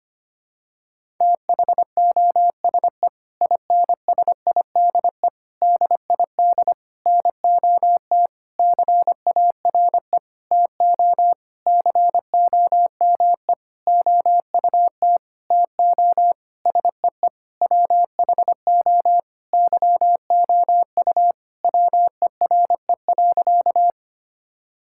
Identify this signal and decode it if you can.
Morse code, heard as a keyed tone, about 25 wpm, 700 Hz. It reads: THOSE INSIDE DID NOT CARE TO COME OUT TO SEE WHO YOU WERE.
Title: Sentences from Top 1000 Words - Review Part 6 - 25wpm